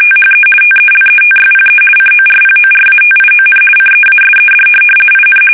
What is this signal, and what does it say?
RTTY, about 50 baud. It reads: THE TELETYPE CORPORATION CEASED IN
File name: fsk850-50B / teletype-corp